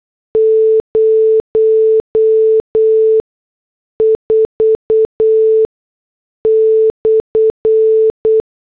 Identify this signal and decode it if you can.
Morse code, heard as a keyed tone, about 8 wpm, spaced as for 6 wpm, 440 Hz. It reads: 04/